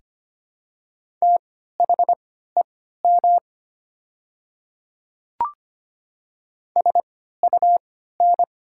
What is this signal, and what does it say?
Morse code, heard as a keyed tone, about 25 wpm, spaced as for 14 wpm, 700 Hz. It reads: THEM SUN
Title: 3x Character Spacing - Top 500 Words - 25wpm